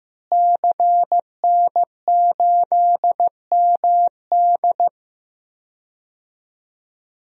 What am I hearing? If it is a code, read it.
Morse code, heard as a keyed tone, about 15 wpm, 700 Hz. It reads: CN8MD
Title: DXCC-100 Call Signs - 5 Characters - Repeated 2x - 15wpm